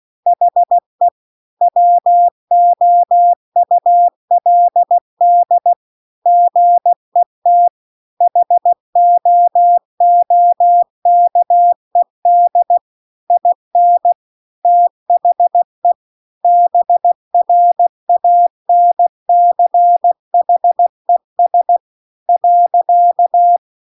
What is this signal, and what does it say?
Morse code, heard as a keyed tone, about 16 wpm, 700 Hz. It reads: HE WOULD GET HOOKED IN THE BRANCHES .